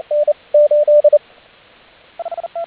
Morse code, heard as a keyed tone, about 30 wpm, 590 Hz, with radio noise. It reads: N 8